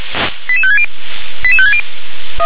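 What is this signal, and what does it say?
Five-tone selective call (ZVEI), heard as quick runs of five tones, tones of about 70 ms. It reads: 88560 88560
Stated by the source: ZVEI five-tone selective call received with an SDR.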